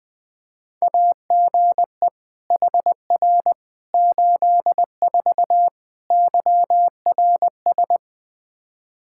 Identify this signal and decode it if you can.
Morse code, heard as a keyed tone, about 20 wpm, 700 Hz. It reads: AGE HR 84 YRS
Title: Practice QSO - Exchange 3 - More Info - 20wpm